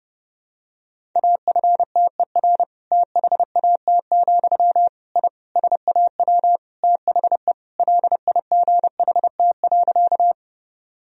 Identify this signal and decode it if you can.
Morse code, heard as a keyed tone, about 30 wpm, 700 Hz. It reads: AFTER THAT, I SAW THE LIGHT.
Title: Sentences from Top 1000 Words - Review Part 1 - 30wpm